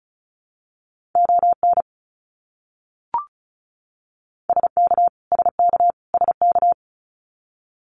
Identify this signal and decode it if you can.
Morse code, heard as a keyed tone, about 34 wpm, 700 Hz. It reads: ON SK SK SK
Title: Canadian Provinces and Territories - Repeated 3x - 35wpm